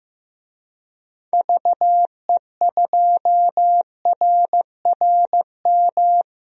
Morse code, heard as a keyed tone, about 15 wpm, 700 Hz. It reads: VE2RRM
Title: DXCC-100 Call Signs - 6 Characters - Repeated 2x - 15wpm